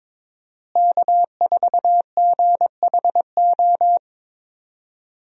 Morse code, heard as a keyed tone, about 22 wpm, 700 Hz. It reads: K4GHO
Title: USA Call Signs - 5 Characters - Repeated 2x - 22wpm